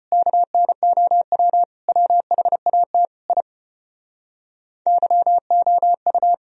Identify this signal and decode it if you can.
Morse code, heard as a keyed tone, about 32 wpm, 700 Hz. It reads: KNOW WHAT I YOU